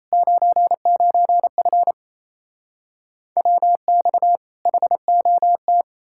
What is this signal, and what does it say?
Morse code, heard as a keyed tone, about 37 wpm, 700 Hz. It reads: 99F WX HOT